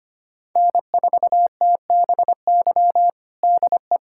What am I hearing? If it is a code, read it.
Morse code, heard as a keyed tone, about 25 wpm, 700 Hz. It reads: N4TBY DE
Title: USA Callsign DE Callsign - 5 Characters - 25wpm